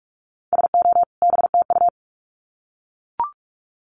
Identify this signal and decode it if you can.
Morse code, heard as a keyed tone, about 44 wpm, 700 Hz. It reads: SO BTU
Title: Common QSO Phrases 45wpm